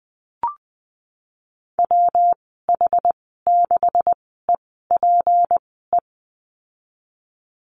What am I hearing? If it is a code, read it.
Morse code, heard as a keyed tone, about 19 wpm, spaced as for 14 wpm, 700 Hz. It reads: WH6EPE
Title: 173 Lesson 28g - Number 9 - Call Signs 20wpm